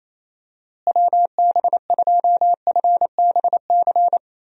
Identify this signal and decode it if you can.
Morse code, heard as a keyed tone, about 28 wpm, 700 Hz. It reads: WB2FBC
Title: USA Call Signs - 6 Characters - Repeated 3x - 28wpm